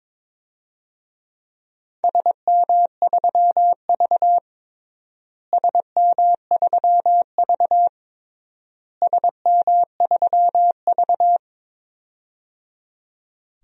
Morse code, heard as a keyed tone, about 21 wpm, 700 Hz. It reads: SM3V SM3V SM3V
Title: DXCC-100 Call Signs - 4 Characters - Repeated 3x - 22wpm